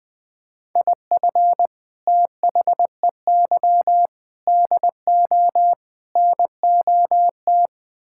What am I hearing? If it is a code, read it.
Morse code, heard as a keyed tone, about 20 wpm, 700 Hz. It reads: IF THEY DO NOT
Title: Build A Phrase - 4-Word Phrases - 20wpm